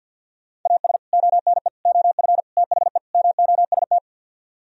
Morse code, heard as a keyed tone, about 64 wpm, 700 Hz. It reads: AS ONE OF THE MOST